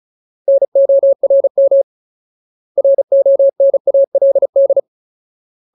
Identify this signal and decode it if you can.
Morse code, heard as a keyed tone, about 34 wpm, 550 Hz. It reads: NORM RONALD
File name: Ultimate Word List / 35WPM Ultimate